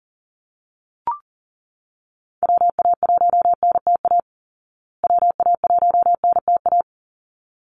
Morse code, heard as a keyed tone, about 40 wpm, 700 Hz. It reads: WA1NTA WA1NTA
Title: Call Signs Repeated 3x - 40wpm